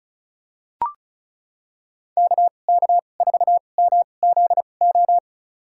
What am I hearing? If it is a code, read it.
Morse code, heard as a keyed tone, about 36 wpm, spaced as for 25 wpm, 700 Hz. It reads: KK4MZO
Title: USA Call Signs - 6 Characters - 2x Character Spacing - 35wpm